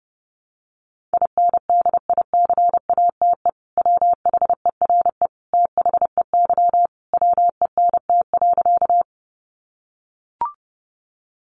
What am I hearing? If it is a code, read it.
Morse code, heard as a keyed tone, about 30 wpm, 700 Hz. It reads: INDICATE WHERE THEY WENT.
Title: Sentences from Top 1300 Words 30wpm